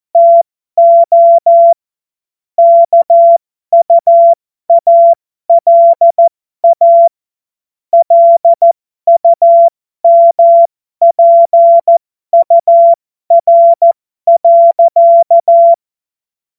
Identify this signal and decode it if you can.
Morse code, heard as a keyed tone, about 14 wpm, spaced as for 12 wpm, 680 Hz. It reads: TO KUALA LUMPUR.
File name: Morsekurs / Witze